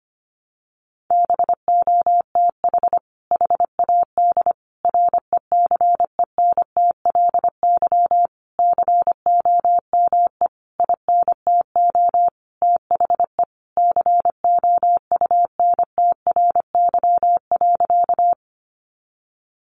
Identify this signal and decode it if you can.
Morse code, heard as a keyed tone, about 25 wpm, 700 Hz. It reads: BOTH HAD RECENTLY COME INTO THE COUNTRY.
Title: Sentences from Top 1200 Words 25wpm